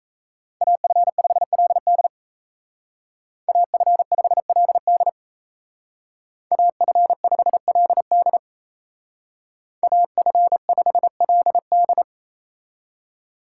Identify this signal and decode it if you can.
Morse code, heard as a keyed tone, about 36 wpm, 700 Hz. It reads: AF5LD AF5LD AF5LD AF5LD